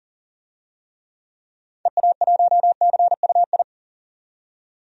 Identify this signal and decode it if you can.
Morse code, heard as a keyed tone, about 40 wpm, 700 Hz. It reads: EA1CUI
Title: DXCC-100 Call Signs - 6 Characters - 40wpm